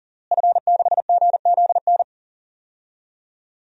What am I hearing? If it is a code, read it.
Morse code, heard as a keyed tone, about 40 wpm, 700 Hz. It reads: F6GZN
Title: DXCC-100 Call Signs - 5 Characters - Repeated 3x - 40wpm